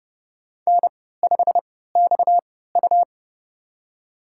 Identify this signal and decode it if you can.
Morse code, heard as a keyed tone, about 32 wpm, spaced as for 17 wpm, 700 Hz. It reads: N5XU
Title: USA Call Signs - 4 Characters - 3x Character Spacing - 30wpm